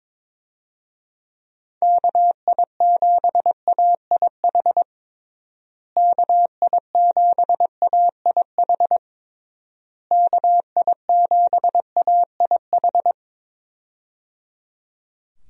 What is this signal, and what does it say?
Morse code, heard as a keyed tone, about 22 wpm, 700 Hz. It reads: KI7AIH KI7AIH KI7AIH
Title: USA Call Signs - 6 Characters - Repeated 3x - 22wpm